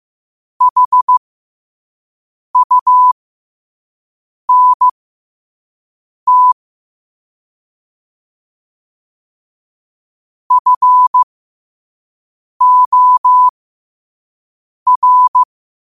Morse code, heard as a keyed tone, about 15 wpm, spaced as for 5 wpm, 1000 Hz. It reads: HUNT FOR
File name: qotdfiles / 5wpm